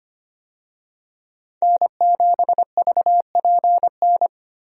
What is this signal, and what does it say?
Morse code, heard as a keyed tone, about 25 wpm, 700 Hz. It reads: N7VPN
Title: USA Call Signs - 5 Characters - 25wpm